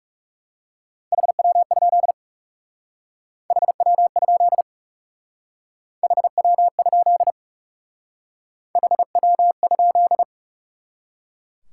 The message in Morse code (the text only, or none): HW? HW? HW? HW?